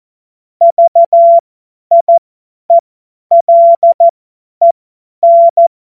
Morse code, heard as a keyed tone, about 14 wpm, spaced as for 10 wpm, 680 Hz. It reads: VIELEN
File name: Morsekurs / Witze / Witz 24. Januar 2023 50 F70